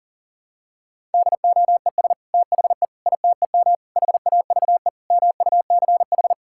Morse code, heard as a keyed tone, about 40 wpm, 700 Hz. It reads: DOES THE ITEM HAVE MUCH
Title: Sentences from Top 1000 Words - Review Part 6 - 40wpm